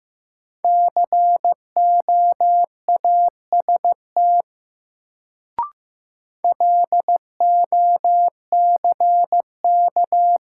Morse code, heard as a keyed tone, about 15 wpm, 700 Hz. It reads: COAST LOCK
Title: Sentences from Top 1300 Words - New Word Drill Alt - 15wpm